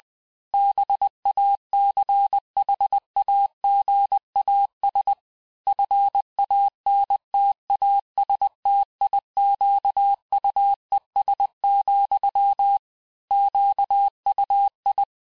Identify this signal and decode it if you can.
Morse code, heard as a keyed tone, about 20 wpm, 800 Hz. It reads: BACHAGAS FANTASTIQUES, QUI
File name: cours-du-mercredi / audios / cours-858-texte-clais-vitesse-19mpm